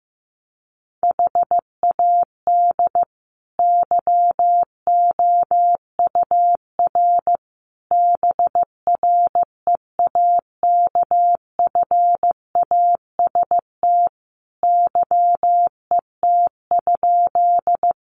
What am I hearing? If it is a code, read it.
Morse code, heard as a keyed tone, about 15 wpm, 700 Hz. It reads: HAD YOUR BREAKFAST YET?
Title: Sentences from Top 1500 Words 15wpm